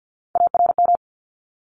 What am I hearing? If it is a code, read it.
Morse code, heard as a keyed tone, about 50 wpm, 700 Hz. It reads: ARM
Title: ES:S Suffix - Top 100 Words with Base Words 50wpm